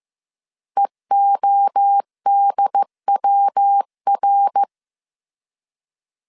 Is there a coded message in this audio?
Yes: Morse code from 0.8 to 4.6 s, heard as a keyed tone, about 15 wpm, 800 Hz. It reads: EODWR